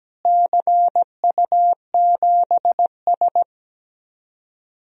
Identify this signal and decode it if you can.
Morse code, heard as a keyed tone, about 17 wpm, 700 Hz. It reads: CU7S
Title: DXCC-100 Call Signs - 4 Characters - Repeated 2x - 17wpm